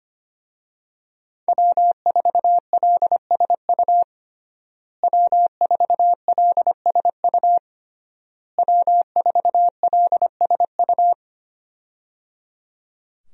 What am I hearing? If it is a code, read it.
Morse code, heard as a keyed tone, about 25 wpm, 700 Hz. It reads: W4LSU W4LSU W4LSU
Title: USA Call Signs - 5 Characters - Repeated 3x - 25wpm